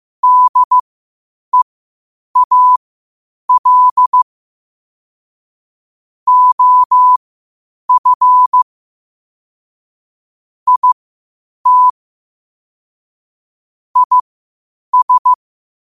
Morse code, heard as a keyed tone, about 15 wpm, spaced as for 8 wpm, 1000 Hz. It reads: DEAL OF IT IS
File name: qotdfiles / 8wpm